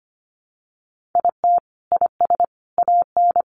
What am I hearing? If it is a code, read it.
Morse code, heard as a keyed tone, about 25 wpm, 700 Hz. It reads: IT IS AN
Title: Sentences from Top 1500 Words 25wpm